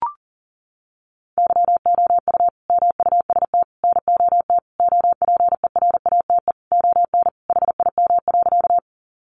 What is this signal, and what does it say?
Morse code, heard as a keyed tone, about 40 wpm, 700 Hz. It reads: YOU MUST NOT OPERATE ON HIM.
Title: Sentences from Top 1500 Words 40wpm